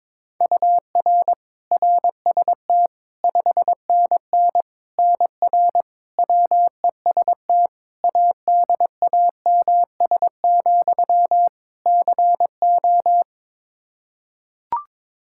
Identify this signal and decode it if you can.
Morse code, heard as a keyed tone, about 22 wpm, 700 Hz. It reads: UR RST 5NN NR WEST ADAMS, CO
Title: Practice QSO - Exchange 1 - Beginning - 22wpm